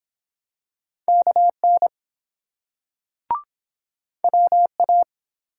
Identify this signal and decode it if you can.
Morse code, heard as a keyed tone, about 24 wpm, 700 Hz. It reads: KN WA